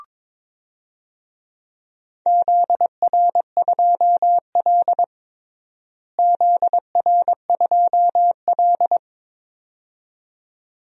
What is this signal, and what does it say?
Morse code, heard as a keyed tone, about 22 wpm, 700 Hz. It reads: ZR2L ZR2L
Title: DXCC-100 Call Signs - 4 Characters - Repeated 2x - 22wpm